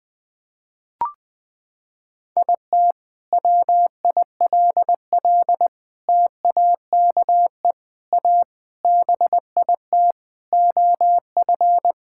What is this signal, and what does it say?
Morse code, heard as a keyed tone, about 20 wpm, 700 Hz. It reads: IT WILL TAKE A BIT OF
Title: Sentences from Top 1000 Words - Review Part 5 - 20wpm